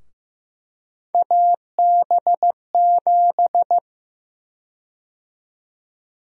Morse code, heard as a keyed tone, about 15 wpm, 700 Hz. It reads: AB7
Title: USA Call Sign Prefixes - 15wpm